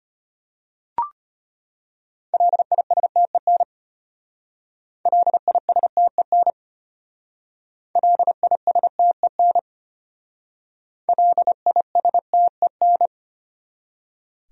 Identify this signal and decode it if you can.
Morse code, heard as a keyed tone, about 32 wpm, 700 Hz. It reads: LISTEN LISTEN LISTEN LISTEN